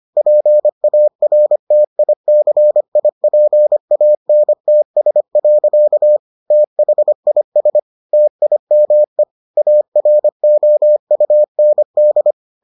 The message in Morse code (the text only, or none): PARTICIPANTS. THIS TIME AROUND